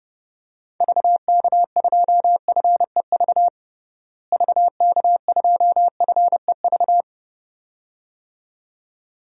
Morse code, heard as a keyed tone, about 30 wpm, 700 Hz. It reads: VK2FEV VK2FEV
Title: DXCC-100 Call Signs - 6 Characters - Repeated 2x - 30wpm